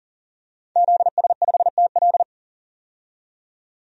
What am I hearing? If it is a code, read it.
Morse code, heard as a keyed tone, about 40 wpm, 700 Hz. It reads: ZS5TL
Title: DXCC-100 Call Signs - 5 Characters - 40wpm